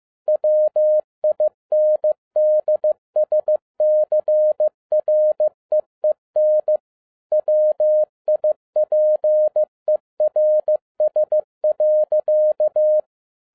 Morse code, heard as a keyed tone, about 15 wpm, 610 Hz. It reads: WINDSCREEN WIPERS.
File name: 2012-01-mp3 / n07-ARA JR0DIL-15wpm